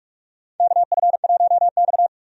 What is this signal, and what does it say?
Morse code, heard as a keyed tone, about 46 wpm, 700 Hz. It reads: KF1X
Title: USA Call Signs - 4 Characters - 45wpm